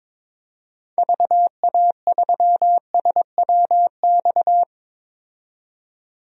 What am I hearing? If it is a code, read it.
Morse code, heard as a keyed tone, about 21 wpm, 700 Hz. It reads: VA3SWX
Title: DXCC-100 Call Signs - 6 Characters - 22wpm